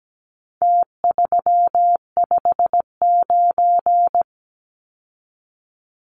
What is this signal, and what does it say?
Morse code, heard as a keyed tone, about 17 wpm, 700 Hz. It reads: T359
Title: Numbers - Contest Style - 4-Digit 17wpm